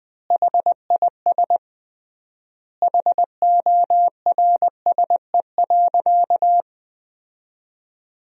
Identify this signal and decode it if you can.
Morse code, heard as a keyed tone, about 20 wpm, 700 Hz. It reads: HIS HORSE.
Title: 3x Word-Spacing - Sentences from Top 300 Words - 20wpm